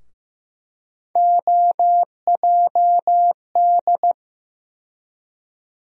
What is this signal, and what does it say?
Morse code, heard as a keyed tone, about 15 wpm, 700 Hz. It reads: OJD
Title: USA Call Sign Postfixes - 3 Letters 15wpm